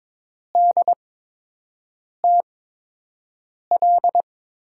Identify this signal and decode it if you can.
Morse code, heard as a keyed tone, about 22 wpm, spaced as for 6 wpm, 700 Hz. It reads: DTL
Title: USA Call Signs - 8x Character Spacing - 22wpm